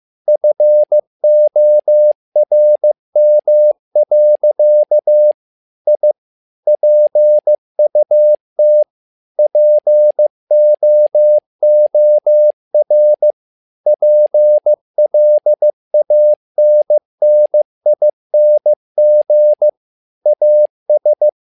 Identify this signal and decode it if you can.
Morse code, heard as a keyed tone, about 15 wpm, 600 Hz. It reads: FORM. I PUT POOR PLANNING AS
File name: learning center / tower trouble / tower trouble 15wpm